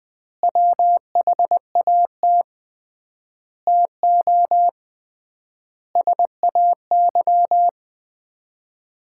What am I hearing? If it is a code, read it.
Morse code, heard as a keyed tone, about 20 wpm, 700 Hz. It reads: WHAT TO SAY
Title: 3x Word-Spacing - Sets of 5 Words - 20wpm